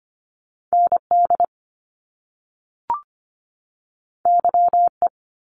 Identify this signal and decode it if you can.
Morse code, heard as a keyed tone, about 25 wpm, 700 Hz. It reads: ND YE
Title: Top Two Letter Combinations in Words 25wpm